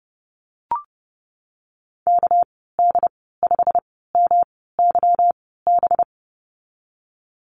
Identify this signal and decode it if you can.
Morse code, heard as a keyed tone, about 30 wpm, spaced as for 17 wpm, 700 Hz. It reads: KD5MYB
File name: Lesson 37g - Letter x - Call Signs 30wpm